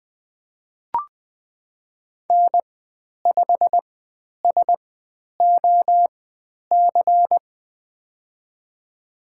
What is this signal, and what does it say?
Morse code, heard as a keyed tone, about 20 wpm, spaced as for 10 wpm, 700 Hz. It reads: N5SOC
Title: 095 Lesson 16g - Letter c - Call Signs 20wpm